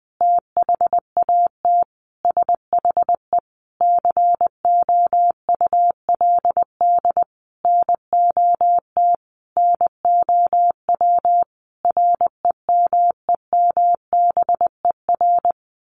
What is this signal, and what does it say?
Morse code, heard as a keyed tone, about 20 wpm, 700 Hz. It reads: THAT SHE COULD NOT NOW REMEMBER
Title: Sentences from Top 1200 Words 20wpm